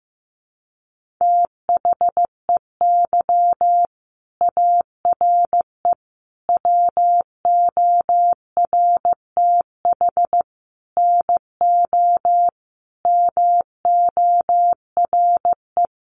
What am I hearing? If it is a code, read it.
Morse code, heard as a keyed tone, about 14 wpm, 700 Hz. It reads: THEY ARE WORTH NO MORE
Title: Sentences from Top 800 Words 15wpm